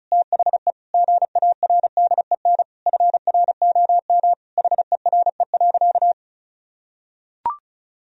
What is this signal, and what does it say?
Morse code, heard as a keyed tone, about 34 wpm, 700 Hz. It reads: THE GARDEN FROM HERE.
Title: Sentences from Top 1000 Words - Review Part 4 - 35wpm